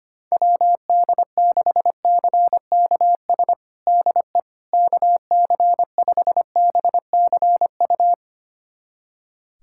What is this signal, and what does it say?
Morse code, heard as a keyed tone, about 25 wpm, 700 Hz. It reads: WD6CKS DE KC5BCU
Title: USA Callsign DE Callsign - 6 Characters - 25wpm